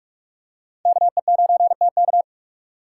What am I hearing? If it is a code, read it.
Morse code, heard as a keyed tone, about 45 wpm, 700 Hz. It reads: KE9TK
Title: USA Call Signs - 5 Characters - 3x Character Spacing - 45wpm